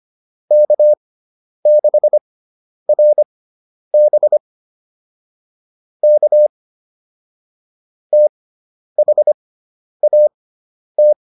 Morse code, heard as a keyed tone, about 25 wpm, spaced as for 10 wpm, 600 Hz. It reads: K6RB K THAT